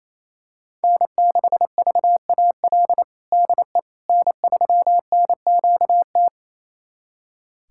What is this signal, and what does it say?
Morse code, heard as a keyed tone, about 28 wpm, 700 Hz. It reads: N6VAL DE N3NQT
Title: USA Callsign DE Callsign - 5 Characters - 28wpm